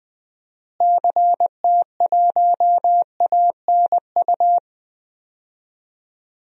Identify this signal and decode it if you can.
Morse code, heard as a keyed tone, about 21 wpm, 700 Hz. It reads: CT1ANU
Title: DXCC-100 Call Signs - 6 Characters - Repeated 3x - 20wpm